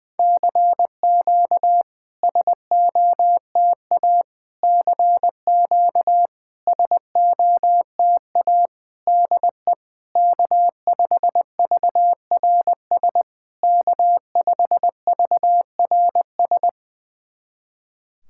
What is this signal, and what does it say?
Morse code, heard as a keyed tone, about 20 wpm, 700 Hz. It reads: CQ SOTA CQ SOTA DE K5VRS K5VRS
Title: SOTA QSO Practice 20wpm